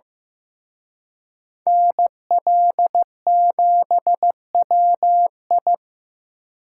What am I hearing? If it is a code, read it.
Morse code, heard as a keyed tone, about 18 wpm, 700 Hz. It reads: NL7WI
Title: Speed Racing - USA 2x2 Call Signs 15wpm